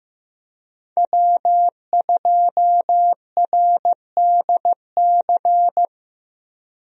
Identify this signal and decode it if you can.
Morse code, heard as a keyed tone, about 15 wpm, 700 Hz. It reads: W2RDC
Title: USA Call Signs - 5 Characters - Repeated 3x - 15wpm